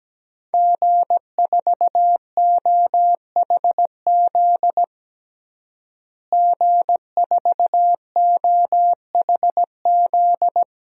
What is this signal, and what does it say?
Morse code, heard as a keyed tone, about 17 wpm, 700 Hz. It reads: G4OHZ G4OHZ
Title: DXCC-100 Call Signs - 5 Characters - Repeated 3x - 17wpm